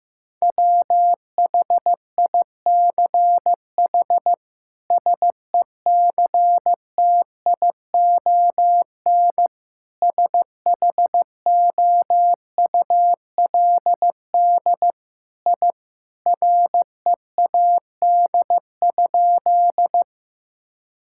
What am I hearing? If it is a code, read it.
Morse code, heard as a keyed tone, about 15 wpm, 700 Hz. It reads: WHICH SECTION SHOULD I READ?
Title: Sentences from Top 1000 Words - Review Part 5 - 15wpm